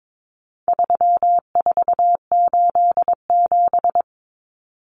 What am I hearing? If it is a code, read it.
Morse code, heard as a keyed tone, about 22 wpm, 700 Hz. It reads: 3487
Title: Numbers - 4-Digit 22wpm